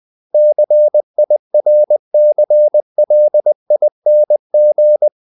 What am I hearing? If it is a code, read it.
Morse code, heard as a keyed tone, about 20 wpm, 600 Hz. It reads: CIRCLING
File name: the old man and the sea part 2 20wpm